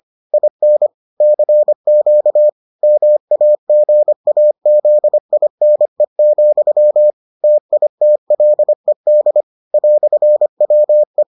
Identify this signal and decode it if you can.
Morse code, heard as a keyed tone, about 25 wpm, 600 Hz. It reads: IN CQ MAGAZINE, TITLED "WE